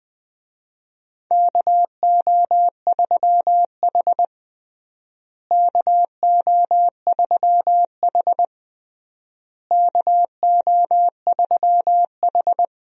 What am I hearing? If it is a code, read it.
Morse code, heard as a keyed tone, about 20 wpm, 700 Hz. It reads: KO3H KO3H KO3H
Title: USA Call Signs - 4 Characters - Repeated 3x - 20wpm